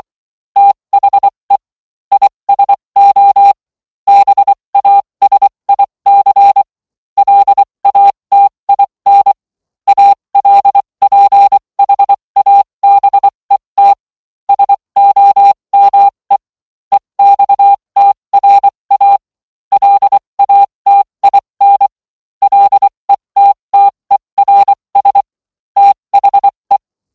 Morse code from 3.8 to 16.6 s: BASIC LATIN ALPHABET SOME